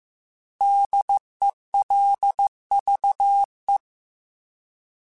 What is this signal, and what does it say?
Morse code, heard as a keyed tone, about 15 wpm, 800 Hz. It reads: DELVE